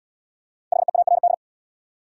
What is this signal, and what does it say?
Morse code, heard as a keyed tone, about 65 wpm, 700 Hz. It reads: HEARD